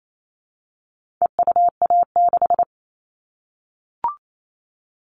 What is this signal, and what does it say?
Morse code, heard as a keyed tone, about 28 wpm, 700 Hz. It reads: EUA6
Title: All Cut Numbers - 4-Digit 28wpm